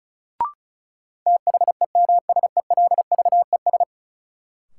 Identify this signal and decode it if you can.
Morse code, heard as a keyed tone, about 36 wpm, 700 Hz. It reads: THEMSELVES
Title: Sentences from Top 800 Words - New Word Drill - 35wpm